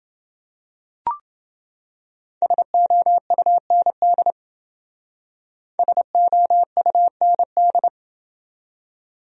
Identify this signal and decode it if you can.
Morse code, heard as a keyed tone, about 30 wpm, 700 Hz. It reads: SOUND SOUND